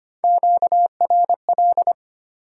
Morse code, heard as a keyed tone, about 32 wpm, 690 Hz. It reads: QRL